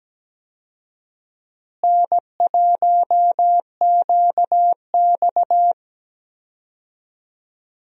Morse code, heard as a keyed tone, about 17 wpm, 700 Hz. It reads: N1QX
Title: USA Call Signs - 4 Characters - Repeated 3x - 17wpm